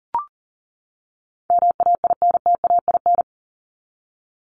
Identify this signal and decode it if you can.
Morse code, heard as a keyed tone, about 40 wpm, 700 Hz. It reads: MAINTAIN
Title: Top 501-750 Words 40wpm